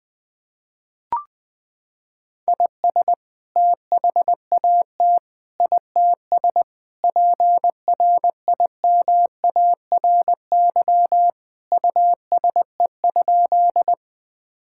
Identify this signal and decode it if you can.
Morse code, heard as a keyed tone, about 21 wpm, 700 Hz. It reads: IS THAT ITS PRIMARY USE?
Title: Sentences from Top 1000 Words - Review Part 5 - 20wpm